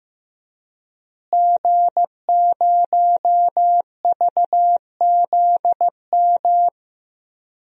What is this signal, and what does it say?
Morse code, heard as a keyed tone, about 15 wpm, 700 Hz. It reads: G0VZM
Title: DXCC-100 Call Signs - 5 Characters - Repeated 3x - 15wpm